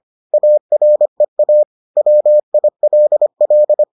Morse code, heard as a keyed tone, about 25 wpm, 600 Hz. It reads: AREA WILL